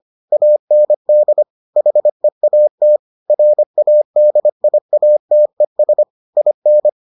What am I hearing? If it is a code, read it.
Morse code, heard as a keyed tone, about 25 wpm, 600 Hz. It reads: AND HEAT RADIATES IN